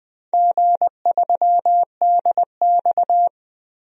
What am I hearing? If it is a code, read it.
Morse code, heard as a keyed tone, about 20 wpm, 700 Hz. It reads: G3DX
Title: DXCC-100 Call Signs - 4 Characters - Repeated 3x - 20wpm